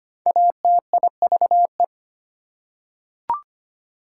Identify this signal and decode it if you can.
Morse code, heard as a keyed tone, about 25 wpm, 700 Hz. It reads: ATIVE
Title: Prefixes and Suffixes 25wpm